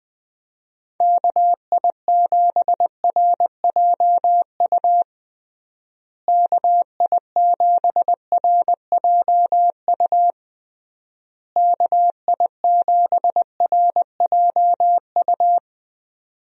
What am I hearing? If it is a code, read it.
Morse code, heard as a keyed tone, about 21 wpm, 700 Hz. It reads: KI7RJU KI7RJU KI7RJU
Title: USA Call Signs - 6 Characters - Repeated 3x - 20wpm